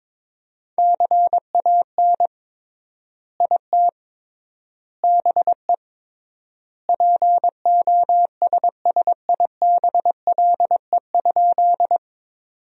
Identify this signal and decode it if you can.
Morse code, heard as a keyed tone, about 22 wpm, 700 Hz. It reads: CAN IT BE POSSIBLE?
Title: 3x Word-Spacing - Sentences from Top 500 Words - 22wpm